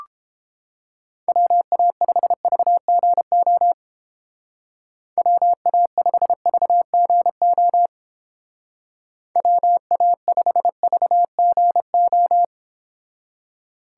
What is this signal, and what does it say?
Morse code, heard as a keyed tone, about 29 wpm, 700 Hz. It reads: WA5VGO WA5VGO WA5VGO